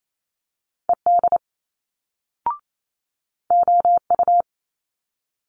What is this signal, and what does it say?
Morse code, heard as a keyed tone, about 28 wpm, 700 Hz. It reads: ED OU
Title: Multi-letter Phonemes Quick Fire 28wpm